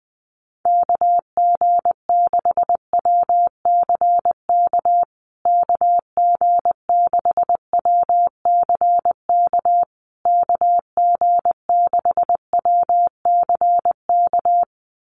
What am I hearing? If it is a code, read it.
Morse code, heard as a keyed tone, about 20 wpm, 700 Hz. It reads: KG6WCK KG6WCK KG6WCK
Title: USA Call Signs Repeated 3x 20wpm